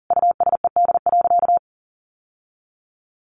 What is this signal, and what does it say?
Morse code, heard as a keyed tone, about 37 wpm, 700 Hz. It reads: USED.
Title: Sentences from Top 700 Words 40wpm